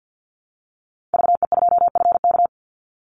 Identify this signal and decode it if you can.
Morse code, heard as a keyed tone, about 50 wpm, 700 Hz. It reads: VE2FK
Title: Call Signs - 50wpm